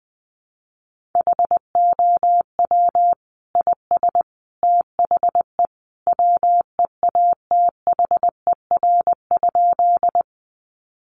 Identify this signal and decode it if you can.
Morse code, heard as a keyed tone, about 20 wpm, 700 Hz. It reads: HOW IS THE WEATHER?
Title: Sentences from Top 800 Words 20wpm